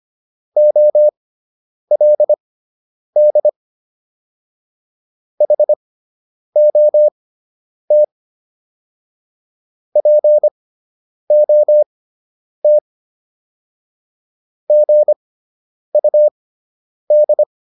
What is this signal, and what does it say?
Morse code, heard as a keyed tone, about 25 wpm, spaced as for 9 wpm, 600 Hz. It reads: OLD HOT POT GUD